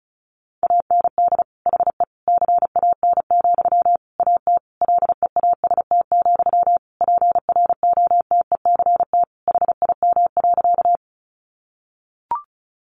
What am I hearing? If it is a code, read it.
Morse code, heard as a keyed tone, about 34 wpm, 700 Hz. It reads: AND HE CAN, AT LEAST, PROTECT HIM.
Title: Sentences from Top 900 Words 35wpm